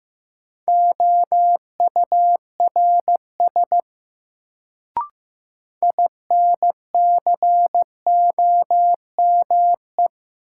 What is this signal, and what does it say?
Morse code, heard as a keyed tone, about 15 wpm, 700 Hz. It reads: OURS INCOME
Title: Sentences from Top 1000 Words - Review Part 5 - New Word Drill - 15wpm